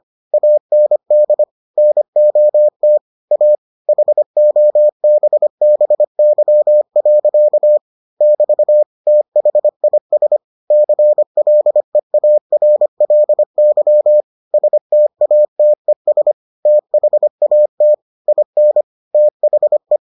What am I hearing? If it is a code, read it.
Morse code, heard as a keyed tone, about 25 wpm, 600 Hz. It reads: AND NOT A HOBBY. <BT> THIS CLEARLY STATES THAT IN THE